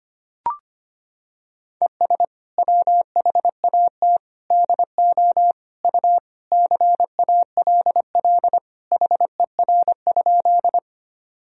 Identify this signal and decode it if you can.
Morse code, heard as a keyed tone, about 25 wpm, 700 Hz. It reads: ES WHAT DO U CALL HER?
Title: Sentences from Top 100 Words and Abbreviations 25wpm